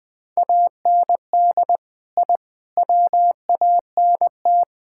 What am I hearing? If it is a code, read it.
Morse code, heard as a keyed tone, about 20 wpm, 700 Hz. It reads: AND I WANT
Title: Sentences with 4-Word Phrases - 20wpm